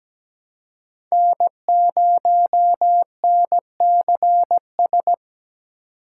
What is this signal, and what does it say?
Morse code, heard as a keyed tone, about 17 wpm, 700 Hz. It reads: N0NCS
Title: USA Call Signs - 5 Characters - 3x Character Spacing - 17wpm